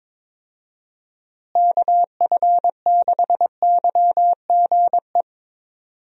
Morse code, heard as a keyed tone, about 22 wpm, 700 Hz. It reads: KF6YGE
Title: USA Call Signs - 6 Characters - Repeated 2x - 22wpm